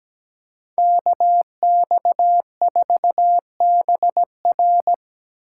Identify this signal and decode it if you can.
Morse code, heard as a keyed tone, about 20 wpm, 700 Hz. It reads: KX4BR
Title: Speed Racing - USA Call Signs 17wpm